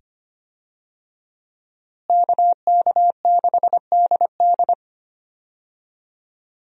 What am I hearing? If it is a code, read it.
Morse code, heard as a keyed tone, about 25 wpm, 700 Hz. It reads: KK6DD
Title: USA Call Signs - 5 Characters - Repeated 2x - 25wpm